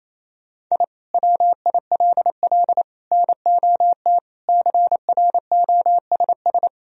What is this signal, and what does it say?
Morse code, heard as a keyed tone, about 28 wpm, 700 Hz. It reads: I WILL NOT CROSS
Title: Sentences from Top 1000 Words - Review Part 1 - 28wpm